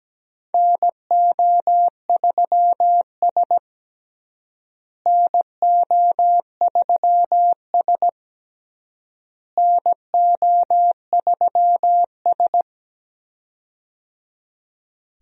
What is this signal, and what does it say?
Morse code, heard as a keyed tone, about 17 wpm, 700 Hz. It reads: NO3S NO3S NO3S
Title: USA Call Signs - 4 Characters - Repeated 3x - 17wpm